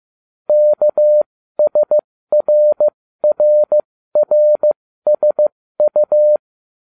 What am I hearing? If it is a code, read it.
Morse code, heard as a keyed tone, about 15 wpm, spaced as for 13 wpm, 600 Hz. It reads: KSRRRSU